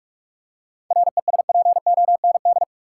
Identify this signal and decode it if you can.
Morse code, heard as a keyed tone, about 46 wpm, 700 Hz. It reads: RESPOND
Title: Seven Letter Words - 45wpm